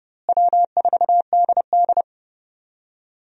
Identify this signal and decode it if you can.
Morse code, heard as a keyed tone, about 30 wpm, 700 Hz. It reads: W4DD
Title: USA Call Signs - 4 Characters - Repeated 3x - 30wpm